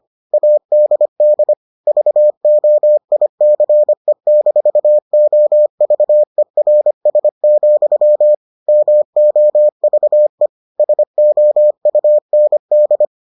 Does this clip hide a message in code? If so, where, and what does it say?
Yes: Morse code from 0.3 to 13.1 s, heard as a keyed tone, about 25 wpm, 600 Hz. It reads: ADD VOICE-OVERS, MOVE SOUND